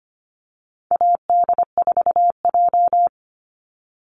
Morse code, heard as a keyed tone, about 25 wpm, 700 Hz. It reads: AD4J
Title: Call Signs Repeated 3x - 25wpm